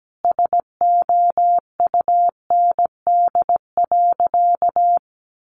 Morse code, heard as a keyed tone, about 17 wpm, 700 Hz. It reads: SOUND.
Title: Sentences from Top 900 Words 17wpm